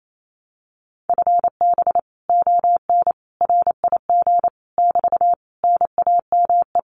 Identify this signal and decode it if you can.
Morse code, heard as a keyed tone, about 28 wpm, 700 Hz. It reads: FB ON RIG <BT> NAME
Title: Common QSO Phrases Encore 28wpm